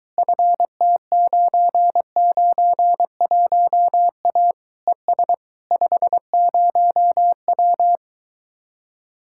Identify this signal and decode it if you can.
Morse code, heard as a keyed tone, about 23 wpm, 700 Hz. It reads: FT991A ES 50W